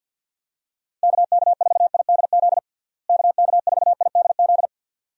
Morse code, heard as a keyed tone, about 48 wpm, 700 Hz. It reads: KK4IDZ KK4IDZ
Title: USA Call Signs - 6 Characters - Repeated 2x - 50wpm